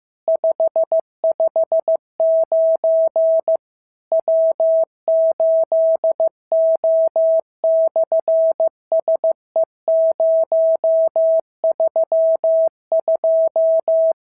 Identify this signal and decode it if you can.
Morse code, heard as a keyed tone, about 15 wpm, 650 Hz. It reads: 559 W8O/SE032
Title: SOTA QSO Practice 15wpm